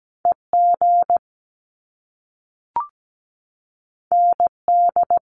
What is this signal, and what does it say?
Morse code, heard as a keyed tone, about 17 wpm, 700 Hz. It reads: EG ND
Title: Top Two Letter Combinations in Words 17wpm